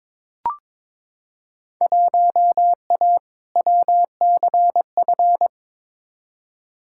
Morse code, heard as a keyed tone, about 22 wpm, 700 Hz. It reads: 1A WCF
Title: Field Day Exchanges - 22wpm